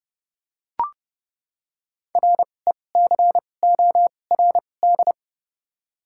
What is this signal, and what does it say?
Morse code, heard as a keyed tone, about 30 wpm, spaced as for 22 wpm, 700 Hz. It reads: RECORD
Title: 2x Character Spacing - Top 500 Words - 30wpm